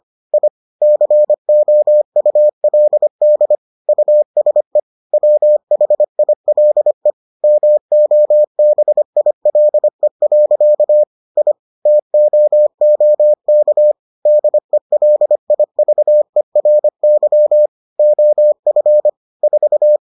Morse code, heard as a keyed tone, about 25 wpm, 600 Hz. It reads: I COULD USE WHILE MOBILE. I TOOK DELIVERY OF 4